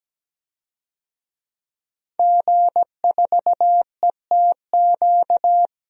Morse code, heard as a keyed tone, about 17 wpm, 700 Hz. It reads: G4ETQ
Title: DXCC-100 Call Signs - 5 Characters - 17wpm